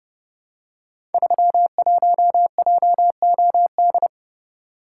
Morse code, heard as a keyed tone, about 30 wpm, 700 Hz. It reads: 31JOD
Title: Code Groups - 5 Letter:Numbers - 30wpm